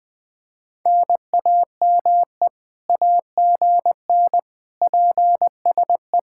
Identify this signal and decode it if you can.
Morse code, heard as a keyed tone, about 20 wpm, 700 Hz. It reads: NAME AGN PSE
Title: Speed-Racing Common QSO Phrases - 17wpm